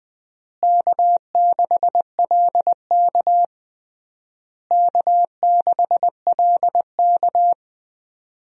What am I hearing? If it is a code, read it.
Morse code, heard as a keyed tone, about 20 wpm, 700 Hz. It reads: K6LK K6LK
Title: USA Call Signs - 4 Characters - Repeated 3x - 20wpm